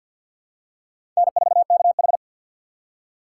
Morse code, heard as a keyed tone, about 50 wpm, 700 Hz. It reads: N4KH
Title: K1USN SST QSO Practice - 50wpm